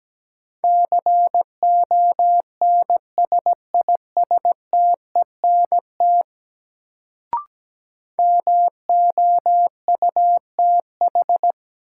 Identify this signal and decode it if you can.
Morse code, heard as a keyed tone, about 17 wpm, 700 Hz. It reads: CONSISTENT MOUTH
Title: Sentences from Top 1500 Words - New Word Drill - 17wpm